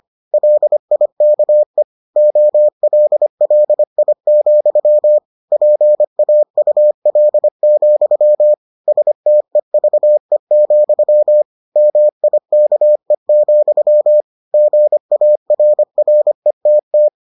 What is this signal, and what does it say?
Morse code, heard as a keyed tone, about 25 wpm, 600 Hz. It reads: LIKE OLLI, PAUL, STEVE, MIKE, GARRETT